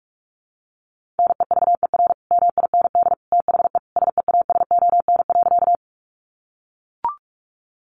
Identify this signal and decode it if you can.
Morse code, heard as a keyed tone, about 44 wpm, 700 Hz. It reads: NEVER MIND THE SEASON.
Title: Sentences from Top 1400 Words 45wpm